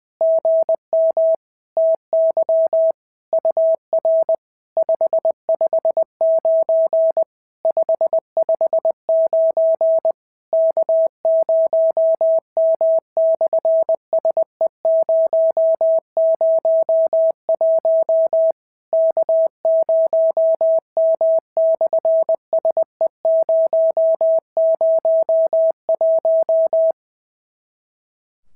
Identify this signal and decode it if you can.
Morse code, heard as a keyed tone, about 22 wpm, 650 Hz. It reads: GM TY UR 559 559 K0M/SE001 K0M/SE001
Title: SOTA QSO Practice 20wpm